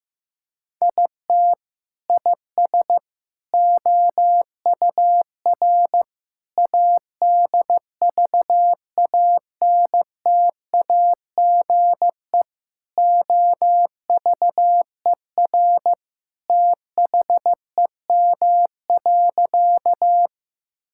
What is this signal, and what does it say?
Morse code, heard as a keyed tone, about 15 wpm, 700 Hz. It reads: IT IS OUR ADVANTAGE OVER THEM.
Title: Sentences from Top 1000 Words - Review Part 6 - 15wpm